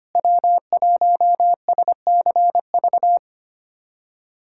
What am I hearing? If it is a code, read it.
Morse code, heard as a keyed tone, about 25 wpm, 700 Hz. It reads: W1SCV
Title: USA Callsign DE Callsign - 5 Characters - 25wpm